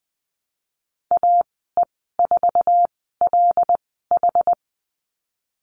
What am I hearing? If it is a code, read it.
Morse code, heard as a keyed tone, about 20 wpm, spaced as for 15 wpm, 700 Hz. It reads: AE4LH
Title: 179 Lesson 29g - Slash - Call Signs 20wpm